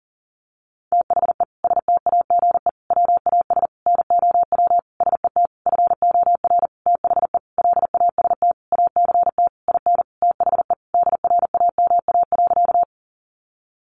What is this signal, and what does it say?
Morse code, heard as a keyed tone, about 40 wpm, 700 Hz. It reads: THE STAGE WAS NOW SET FOR THE LAST ACT IN THE DRAMA.
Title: Sentences from Top 1500 Words 40wpm